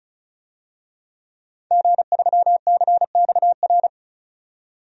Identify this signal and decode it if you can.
Morse code, heard as a keyed tone, about 34 wpm, 700 Hz. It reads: G3CXR
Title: DXCC-100 Call Signs - 5 Characters - 35wpm